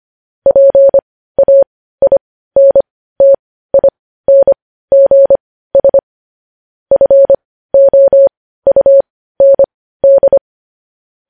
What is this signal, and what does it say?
Morse code, heard as a keyed tone, about 25 wpm, spaced as for 15 wpm, 550 Hz. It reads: PAINTINGS FOUND
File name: podcast / News headlines-1x-25x15